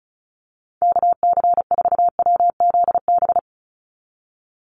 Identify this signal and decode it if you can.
Morse code, heard as a keyed tone, about 34 wpm, 700 Hz. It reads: KC4WZB
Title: NA SOTA Activator Callsigns 35wpm